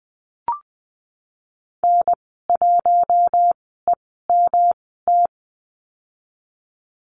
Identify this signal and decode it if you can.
Morse code, heard as a keyed tone, about 20 wpm, spaced as for 15 wpm, 700 Hz. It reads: N1EMT
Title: 147 Lesson 24g - Letter y - Call Signs 20wpm